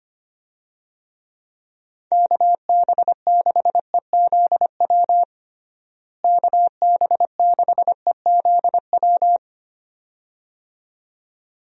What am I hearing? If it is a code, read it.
Morse code, heard as a keyed tone, about 25 wpm, 700 Hz. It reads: KB6EZW KB6EZW
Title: USA Call Signs - 6 Characters - Repeated 2x - 25wpm